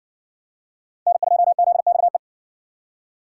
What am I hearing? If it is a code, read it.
Morse code, heard as a keyed tone, about 57 wpm, 700 Hz. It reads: N2ZXE